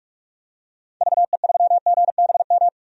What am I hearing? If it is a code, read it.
Morse code, heard as a keyed tone, about 47 wpm, 700 Hz. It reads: VE3GBM
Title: DXCC-100 Call Signs - 6 Characters - 45wpm